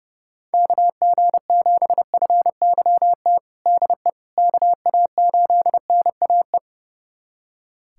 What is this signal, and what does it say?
Morse code, heard as a keyed tone, about 31 wpm, 700 Hz. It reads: KG7FYT DE KA8NAE
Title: USA Callsign DE Callsign - 6 Characters - 30wpm